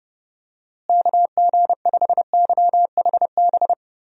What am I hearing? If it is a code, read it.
Morse code, heard as a keyed tone, about 30 wpm, 700 Hz. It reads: KG5YHB
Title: USA Call Signs - 6 Characters - Repeated 2x - 30wpm